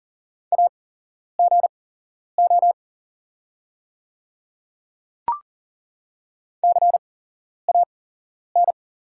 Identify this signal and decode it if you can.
Morse code, heard as a keyed tone, about 40 wpm, spaced as for 11 wpm, 700 Hz. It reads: AGO CAN
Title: 8x Character Spacing - Top 500 Words - 40wpm